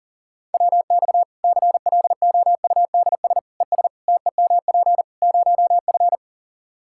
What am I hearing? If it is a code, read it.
Morse code, heard as a keyed tone, about 40 wpm, 700 Hz. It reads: WX CLOUDS ES TEMP 0F
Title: Practice QSO - Exchange 2 - Info - 40wpm